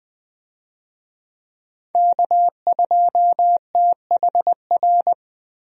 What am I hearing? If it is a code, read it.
Morse code, heard as a keyed tone, about 20 wpm, 700 Hz. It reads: K2THR
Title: USA Call Signs - 5 Characters - 20wpm